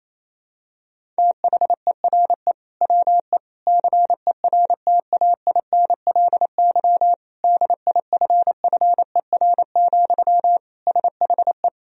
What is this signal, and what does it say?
Morse code, heard as a keyed tone, about 28 wpm, 700 Hz. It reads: THERE WE CERTAINLY DIFFER, SHE
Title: Sentences from Top 1000 Words - Review Part 1 - 28wpm